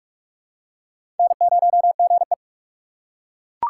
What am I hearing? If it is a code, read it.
Morse code, heard as a keyed tone, about 46 wpm, 700 Hz. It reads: N0GE
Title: USA Call Signs - 4 Characters - 4x Character Spacing - 45wpm